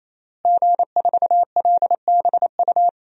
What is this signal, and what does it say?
Morse code, heard as a keyed tone, about 28 wpm, 700 Hz. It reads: G4LBU
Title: DXCC-100 Call Signs - 5 Characters - Repeated 3x - 28wpm